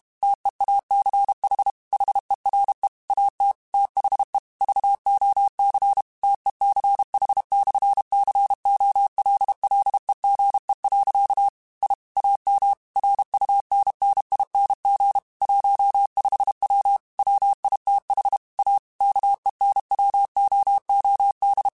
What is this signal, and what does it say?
Morse code, heard as a keyed tone, about 32 wpm, 800 Hz. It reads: TEACH HERE AT THE VOC TECH/COLLEGE. I AM RUNNING 15W WITH A KENWOOD
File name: QSOs / QSOs-mp3s / WD0BGP 1@30